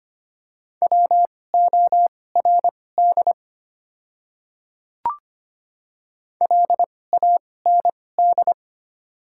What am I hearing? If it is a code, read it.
Morse code, heard as a keyed tone, about 25 wpm, spaced as for 18 wpm, 690 Hz. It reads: WORD LAND
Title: 2x Character Spacing - Top 500 Words - 25wpm